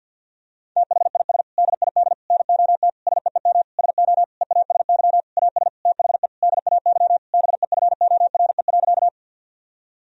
Morse code, heard as a keyed tone, about 53 wpm, 700 Hz. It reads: THIS DID NOT SEEM SO EASY AS THE DAY BEFORE.
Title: Sentences from Top 1000 Words - Review Part 1 - 50wpm